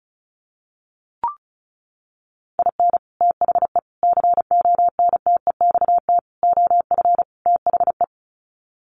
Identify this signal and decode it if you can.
Morse code, heard as a keyed tone, about 34 wpm, 700 Hz. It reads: IN THE CONTEXT OF THE
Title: Sets of 5 Words 35wpm